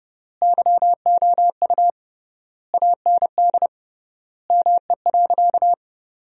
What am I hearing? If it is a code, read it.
Morse code, heard as a keyed tone, about 32 wpm, 700 Hz. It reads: YOU AND ME.
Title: 3x Word-Spacing - Sentences from Top 500 Words - 30wpm